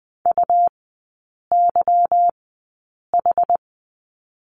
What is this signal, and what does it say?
Morse code, heard as a keyed tone, about 20 wpm, spaced as for 8 wpm, 690 Hz. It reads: UYH
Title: 145 Lesson 24e - Letter y - Three Characters 20wpm